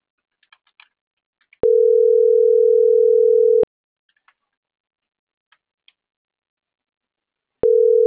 Ringback tone (North America), 440 + 480 Hz.